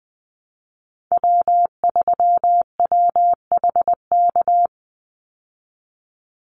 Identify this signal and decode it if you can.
Morse code, heard as a keyed tone, about 20 wpm, 700 Hz. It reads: W3WHK
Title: Call Signs - 20wpm